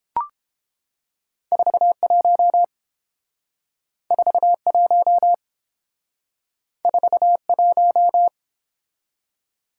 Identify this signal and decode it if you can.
Morse code, heard as a keyed tone, about 29 wpm, 700 Hz. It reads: 41 41 41